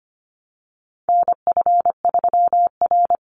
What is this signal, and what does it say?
Morse code, heard as a keyed tone, about 25 wpm, 700 Hz. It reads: NF3R
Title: Call Signs Repeated 2x - 25wpm